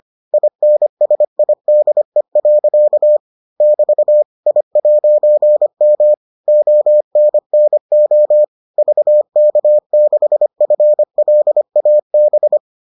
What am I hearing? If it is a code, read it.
Morse code, heard as a keyed tone, about 25 wpm, 600 Hz. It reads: INSIDE. <BT> I'M ONNO VK6FLAB